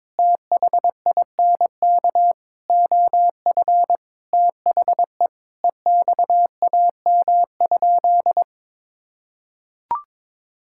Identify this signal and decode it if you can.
Morse code, heard as a keyed tone, about 22 wpm, 700 Hz. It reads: THINK OF THE EXAM?
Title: Sentences from Top 1000 Words - Review Part 5 - 22wpm